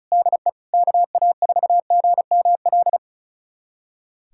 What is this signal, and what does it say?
Morse code, heard as a keyed tone, about 34 wpm, 690 Hz. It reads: DE KA4GML
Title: USA Callsign DE Callsign - 6 Characters - 35wpm